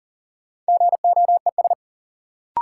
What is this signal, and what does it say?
Morse code, heard as a keyed tone, about 40 wpm, 700 Hz. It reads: GOES
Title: Sentences from Top 1000 Words - Review Part 5 - New Word Drill - 40wpm